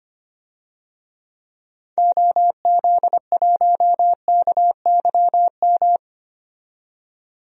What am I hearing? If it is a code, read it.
Morse code, heard as a keyed tone, about 25 wpm, 700 Hz. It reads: OZ1KYM
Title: DXCC-100 Call Signs - 6 Characters - 25wpm